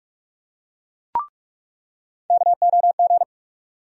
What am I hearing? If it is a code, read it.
Morse code, heard as a keyed tone, about 47 wpm, 700 Hz. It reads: KOG
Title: USA Call Sign Postfixes - 3 Letters 45wpm